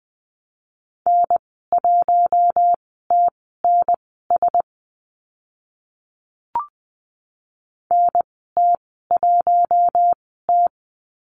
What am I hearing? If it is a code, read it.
Morse code, heard as a keyed tone, about 20 wpm, spaced as for 14 wpm, 700 Hz. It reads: N1TNS NT1T
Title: 042 Lesson 8g - Number 1 - Call Signs 20wpm